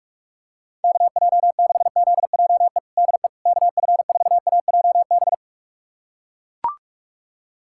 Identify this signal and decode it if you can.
Morse code, heard as a keyed tone, about 42 wpm, 700 Hz. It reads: KJ6ZJE DE KF4AJB
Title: USA Callsign DE Callsign 45wpm